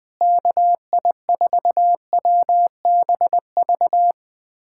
Morse code, heard as a keyed tone, about 20 wpm, 700 Hz. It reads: KI4WBV
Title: USA Callsign DE Callsign - 6 Characters - 20wpm